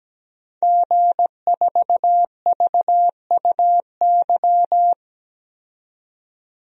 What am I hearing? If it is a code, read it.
Morse code, heard as a keyed tone, about 17 wpm, 700 Hz. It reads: G4VUY
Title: DXCC-100 Call Signs - 5 Characters - 17wpm